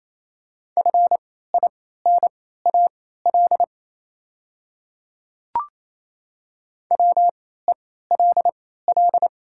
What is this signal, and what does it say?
Morse code, heard as a keyed tone, about 28 wpm, spaced as for 16 wpm, 690 Hz. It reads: FINAL WELL
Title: 3x Character Spacing - Top 500 Words - 28wpm